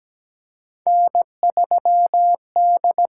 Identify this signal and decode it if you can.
Morse code, heard as a keyed tone, about 17 wpm, 700 Hz. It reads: N3D
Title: USA Call Signs - 4 Characters - 6x Character Spacing - 17wpm